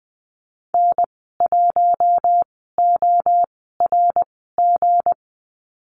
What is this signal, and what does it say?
Morse code, heard as a keyed tone, about 20 wpm, spaced as for 15 wpm, 700 Hz. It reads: N1ORG
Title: 193 Lesson 31g - Letter v - Call Signs 20wpm